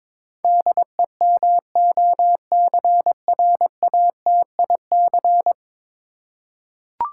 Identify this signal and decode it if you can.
Morse code, heard as a keyed tone, about 22 wpm, 700 Hz. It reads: DEMOCRATIC
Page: morse-code-ninja-practice-22wpm